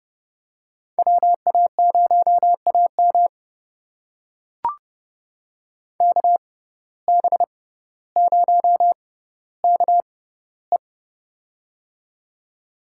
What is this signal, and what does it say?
Morse code, heard as a keyed tone, about 30 wpm, spaced as for 15 wpm, 690 Hz. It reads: WA0AM KB0KE
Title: USA Call Signs - 5 Characters - 6x Character Spacing - 30wpm